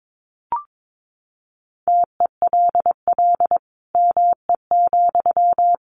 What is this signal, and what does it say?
Morse code, heard as a keyed tone, about 22 wpm, 700 Hz. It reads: TELL ME,
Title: Sentences from Top 1500 Words 22wpm